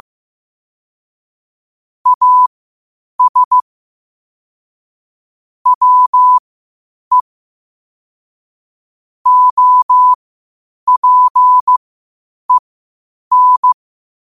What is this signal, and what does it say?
Morse code, heard as a keyed tone, about 15 wpm, spaced as for 8 wpm, 1000 Hz. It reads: AS WE OPEN
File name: qotdfiles / 8wpm